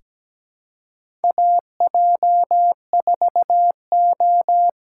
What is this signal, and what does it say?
Morse code, heard as a keyed tone, about 17 wpm, 700 Hz. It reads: AJ4O
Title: USA Call Signs - 5 Characters - 6x Character Spacing - 17wpm